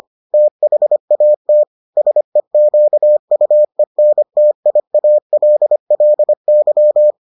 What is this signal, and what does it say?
Morse code, heard as a keyed tone, about 25 wpm, 600 Hz. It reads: THAT SEQUENTIALLY